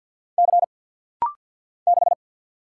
Morse code, heard as a keyed tone, about 50 wpm, 700 Hz. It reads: C 6
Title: Single Letter-Number - Rapid-Fire 50wpm